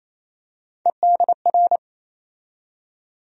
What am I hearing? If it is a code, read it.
Morse code, heard as a keyed tone, about 28 wpm, 700 Hz. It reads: EDR
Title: USA Call Sign Postfixes - 3 Letters 28wpm